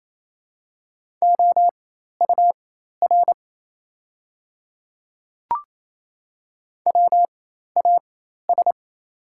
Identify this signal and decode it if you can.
Morse code, heard as a keyed tone, about 28 wpm, spaced as for 13 wpm, 690 Hz. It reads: OUR WAS
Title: 4x Character Spacing - Top 200 Words - 28wpm